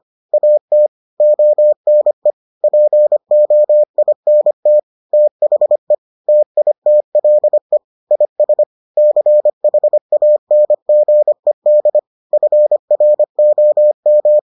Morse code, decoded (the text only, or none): AT ONE POINT THE TITLE IS CHANGED FROM